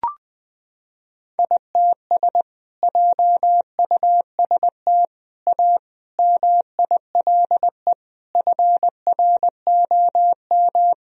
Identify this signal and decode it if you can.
Morse code, heard as a keyed tone, about 21 wpm, 700 Hz. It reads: ITS JUST A MILE FROM
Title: Sentences from Top 1000 Words - Review Part 1 - 20wpm